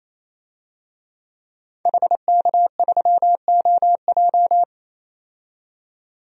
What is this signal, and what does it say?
Morse code, heard as a keyed tone, about 28 wpm, 700 Hz. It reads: HK3OJ
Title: DXCC-100 Call Signs - 5 Characters - 28wpm